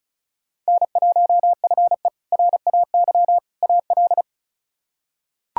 Morse code, heard as a keyed tone, about 36 wpm, 700 Hz. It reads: N1FE RAY AL
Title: K1USN SST QSO Practice - 35wpm